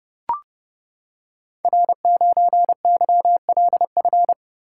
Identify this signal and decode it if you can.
Morse code, heard as a keyed tone, about 30 wpm, 700 Hz. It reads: R9YLF
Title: Code Groups - 5 Letter:Numbers - 30wpm